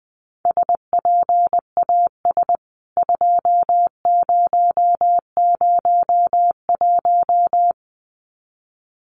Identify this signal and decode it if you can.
Morse code, heard as a keyed tone, about 20 wpm, 700 Hz. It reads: SPAS 2001
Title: CWT Exchange 20wpm